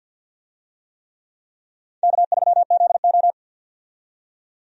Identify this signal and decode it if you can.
Morse code, heard as a keyed tone, about 51 wpm, 700 Hz. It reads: K3ZO
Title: USA Call Signs - 4 Characters - 50wpm